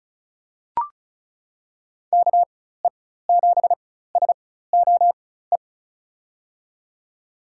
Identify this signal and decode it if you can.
Morse code, heard as a keyed tone, about 34 wpm, spaced as for 16 wpm, 700 Hz. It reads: KE7SOE
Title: USA Call Signs - 6 Characters - 4x Character Spacing - 35wpm